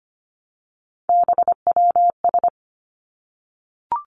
Morse code, heard as a keyed tone, about 25 wpm, 700 Hz. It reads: BWS
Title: Three Challenging Letter-Number Combinations 25wpm